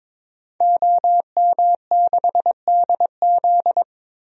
Morse code, heard as a keyed tone, about 22 wpm, 700 Hz. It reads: OM6DZ
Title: DXCC-100 Call Signs - 5 Characters - 22wpm